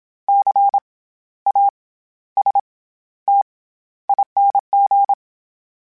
Morse code, heard as a keyed tone, about 27 wpm, 800 Hz. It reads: C A S T ING
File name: Special INGs-mp3s / Special INGs-mp3s / SlowToFaster ing @25